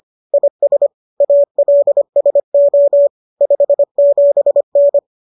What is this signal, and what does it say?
Morse code, heard as a keyed tone, about 25 wpm, 570 Hz. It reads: IS ALSO 57N
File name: qso221 25